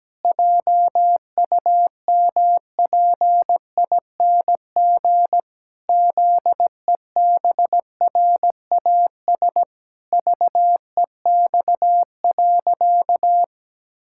Morse code, heard as a keyed tone, about 17 wpm, 700 Hz. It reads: JUMPING ZEBRAS VEX.
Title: Pangram Challenge v2 - 17wpm